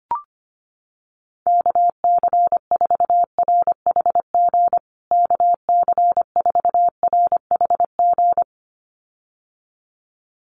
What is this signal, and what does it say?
Morse code, heard as a keyed tone, about 25 wpm, 700 Hz. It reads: KC4RHG KC4RHG
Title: USA Call Signs Repeated 2x 25wpm